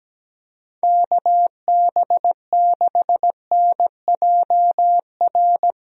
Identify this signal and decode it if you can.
Morse code, heard as a keyed tone, about 17 wpm, 700 Hz. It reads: KB6NJR
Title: USA Call Signs - 6 Characters - Repeated 3x - 17wpm